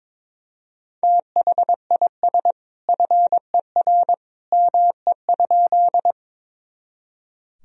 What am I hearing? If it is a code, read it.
Morse code, heard as a keyed tone, about 22 wpm, 700 Hz. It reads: THIS FER ME?
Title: Sentences from Top 200 Words and Abbreviations 22wpm